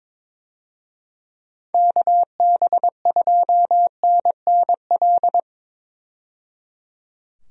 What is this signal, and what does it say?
Morse code, heard as a keyed tone, about 22 wpm, 700 Hz. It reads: KB2NNL
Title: USA Call Signs - 6 Characters - 22wpm